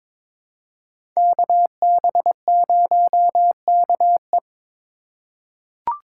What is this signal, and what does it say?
Morse code, heard as a keyed tone, about 22 wpm, 700 Hz. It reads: KB0KE
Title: USA Call Signs - 5 Characters - 4x Character Spacing - 22wpm